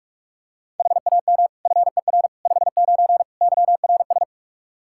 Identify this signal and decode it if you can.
Morse code, heard as a keyed tone, about 44 wpm, 700 Hz. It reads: HAM FER 59 YRS
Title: Practice QSO - Exchange 3 - More Info - 45wpm